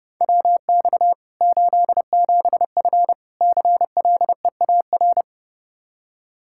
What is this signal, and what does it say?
Morse code, heard as a keyed tone, about 32 wpm, 700 Hz. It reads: WX 87F CLEAR
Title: Practice QSO - Exchange 2 - Info - 30wpm